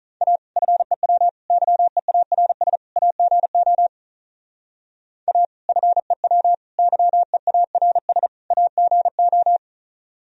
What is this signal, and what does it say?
Morse code, heard as a keyed tone, about 43 wpm, 700 Hz. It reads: A FEW YEARS AGO A FEW YEARS AGO
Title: Speed-Racing - Sets of 4 Words 35wpm